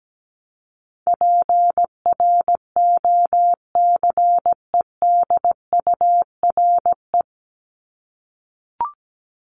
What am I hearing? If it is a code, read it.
Morse code, heard as a keyed tone, about 17 wpm, 700 Hz. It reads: PROCEDURE
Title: Top 1751-2000 Words 17wpm